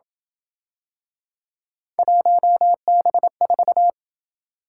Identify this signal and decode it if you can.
Morse code, heard as a keyed tone, about 27 wpm, 700 Hz. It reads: 1B4